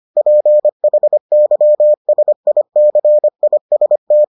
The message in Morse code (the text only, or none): PHYSICIST